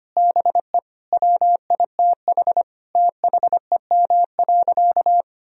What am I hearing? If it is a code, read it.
Morse code, heard as a keyed tone, about 25 wpm, 700 Hz. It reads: BE WITH THEM.
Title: Sentences from Top 1000 Words - Review Part 4 - 25wpm